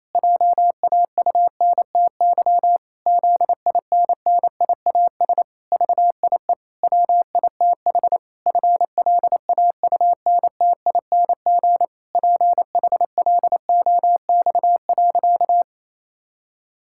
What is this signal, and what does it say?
Morse code, heard as a keyed tone, about 27 wpm, 700 Hz. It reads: JAUNTY ZINNIAS VIE WITH FLAUNTING PHLOX.
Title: Pangram Challenge v2 - 28wpm